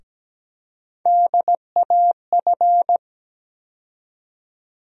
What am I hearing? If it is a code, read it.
Morse code, heard as a keyed tone, about 17 wpm, 700 Hz. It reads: DAF
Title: USA Call Sign Postfixes - 3 Letters 17wpm